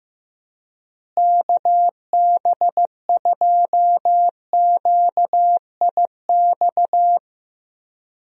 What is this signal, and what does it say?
Morse code, heard as a keyed tone, about 15 wpm, 700 Hz. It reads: KB2QIX
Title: USA Call Signs - 6 Characters - Repeated 3x - 15wpm